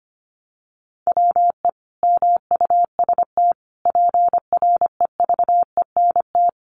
Morse code, heard as a keyed tone, about 25 wpm, 700 Hz. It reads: WE MUST PREVENT
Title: Sentences from Top 1100 Words 25wpm